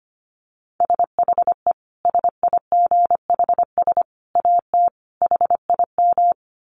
Morse code, heard as a keyed tone, about 25 wpm, 700 Hz. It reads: SHE SIGHS AT HIM
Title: Dit Challenge 25wpm